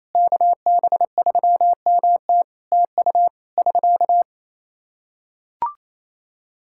Morse code, heard as a keyed tone, about 28 wpm, 700 Hz. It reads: KB3MT TU <SK>
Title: Practice QSO - Exchange 4 - Ending - 28wpm